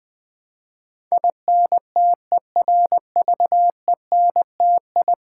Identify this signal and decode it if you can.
Morse code, heard as a keyed tone, about 20 wpm, 700 Hz. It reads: INTERVENTI
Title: Build A Word - 5 Letters At A Time - 12-Letter Words - 20wpm